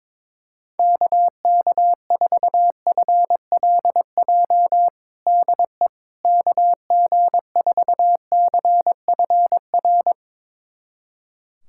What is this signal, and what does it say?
Morse code, heard as a keyed tone, about 21 wpm, 700 Hz. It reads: KK4FLJ DE KG4CFR
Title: USA Callsign DE Callsign - 6 Characters - 22wpm